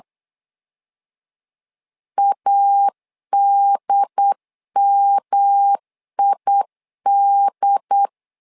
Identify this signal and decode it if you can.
Morse code, heard as a keyed tone, about 8 wpm, 800 Hz. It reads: ADMID
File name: LEZIONI / 3 MWRO / 3 M 8wpm 2300